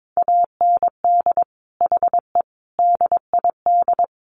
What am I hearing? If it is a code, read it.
Morse code, heard as a keyed tone, about 22 wpm, 700 Hz. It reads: AND HE DID
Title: Sentences from Top 1500 Words 22wpm